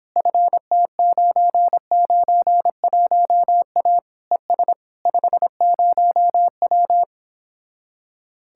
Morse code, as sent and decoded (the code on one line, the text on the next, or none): ..-. - ----. ----. .---- .- / . ... / ..... ----- .--
FT991A ES 50W